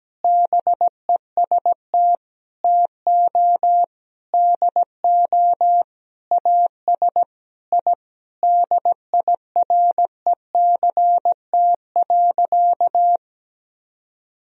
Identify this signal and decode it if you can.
Morse code, heard as a keyed tone, about 17 wpm, 700 Hz. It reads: BEST TO DO AS I DIRECT.
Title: Sentences from Top 1000 Words - Review Part 2 - 17wpm